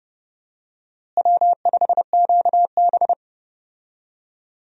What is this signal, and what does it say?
Morse code, heard as a keyed tone, about 30 wpm, 700 Hz. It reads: W5QB
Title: USA Call Signs - 4 Characters - Repeated 3x - 30wpm